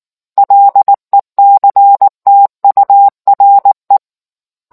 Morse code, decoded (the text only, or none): LECTURE